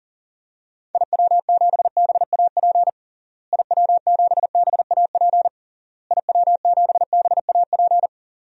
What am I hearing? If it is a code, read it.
Morse code, heard as a keyed tone, about 40 wpm, 700 Hz. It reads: IW7BAP IW7BAP IW7BAP
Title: DXCC-100 Call Signs - 6 Characters - Repeated 3x - 40wpm